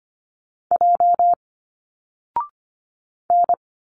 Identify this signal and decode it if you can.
Morse code, heard as a keyed tone, about 24 wpm, 700 Hz. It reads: J N
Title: Single Letters 25wpm